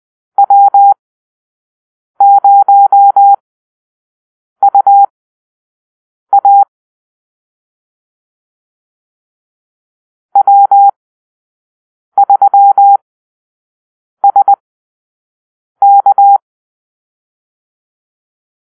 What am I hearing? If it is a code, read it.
Morse code, heard as a keyed tone, about 20 wpm, spaced as for 6 wpm, 800 Hz. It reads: W0UA W3SK
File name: arrl / Call-Sign Drill-5